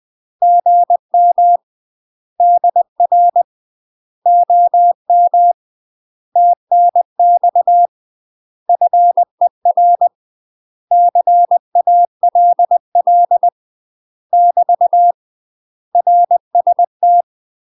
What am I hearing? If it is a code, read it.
Morse code, heard as a keyed tone, about 20 wpm, 700 Hz. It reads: GM DR OM TNX FER CALL <BT> RST
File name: UK MASTER QSOs TRAINING / 20WPM / UK MASTER QSOs 0037